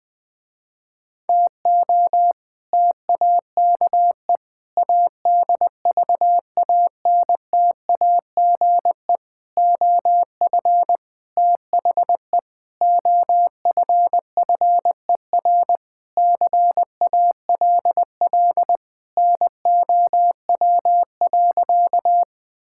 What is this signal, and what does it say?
Morse code, heard as a keyed tone, about 21 wpm, 700 Hz. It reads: TO TAKE ADVANTAGE OF THE OFFER CALL NOW.
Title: Sentences with 4-Word Phrases - Spoken Beginning - 20wpm